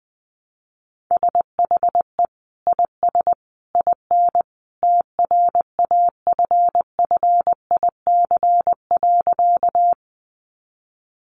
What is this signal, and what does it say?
Morse code, heard as a keyed tone, about 20 wpm, 700 Hz. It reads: SHE IS IN TRAFFIC.
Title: Sentences from Top 1300 Words 20wpm